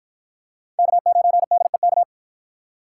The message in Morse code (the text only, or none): K9DEK